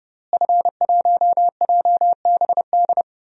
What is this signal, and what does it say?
Morse code, heard as a keyed tone, about 30 wpm, 700 Hz. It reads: F1JBD
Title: DXCC-100 Call Signs - 5 Characters - Repeated 3x - 30wpm